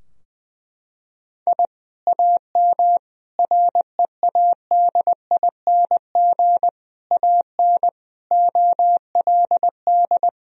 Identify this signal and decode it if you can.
Morse code, heard as a keyed tone, about 20 wpm, 700 Hz. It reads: I AM READING AN OLD
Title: Sentences from Top 1000 Words - Review Part 4 - 20wpm